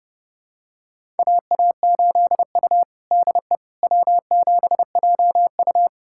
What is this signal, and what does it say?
Morse code, heard as a keyed tone, about 30 wpm, 700 Hz. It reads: AA8U DE W7JU
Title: USA Callsign DE Callsign 30wpm